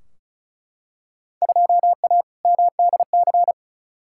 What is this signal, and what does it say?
Morse code, heard as a keyed tone, about 36 wpm, 700 Hz. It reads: 2A MDC
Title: Field Day Exchanges - 35wpm